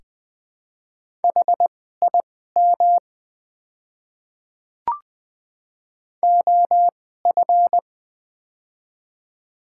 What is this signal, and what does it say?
Morse code, heard as a keyed tone, about 20 wpm, spaced as for 14 wpm, 700 Hz. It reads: HIM OF
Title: 2x Character Spacing - Top 100 Words - 20wpm